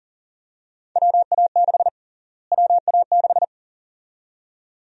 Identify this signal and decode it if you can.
Morse code, heard as a keyed tone, about 40 wpm, 700 Hz. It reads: WA6 WA6
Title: USA Call Sign Prefixes - Repeated 2x - 40wpm